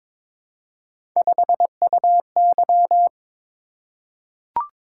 Morse code, heard as a keyed tone, about 22 wpm, 700 Hz. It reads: 5UY
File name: practice / 22wpm / Three Letter-Number Combinations - 22wpm